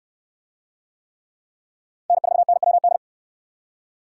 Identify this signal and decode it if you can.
Morse code, heard as a keyed tone, about 69 wpm, 700 Hz. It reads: N4NWD